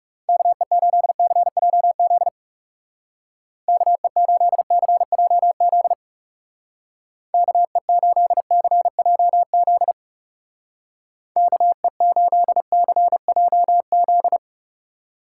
Morse code, heard as a keyed tone, about 36 wpm, 700 Hz. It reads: KE8CJZ KE8CJZ KE8CJZ KE8CJZ
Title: Speed Racing - USA 2x3 Call Signs 30wpm